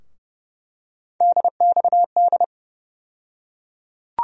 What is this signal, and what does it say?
Morse code, heard as a keyed tone, about 30 wpm, 700 Hz. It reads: DXD
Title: USA Call Sign Postfixes - 3 Letters 30wpm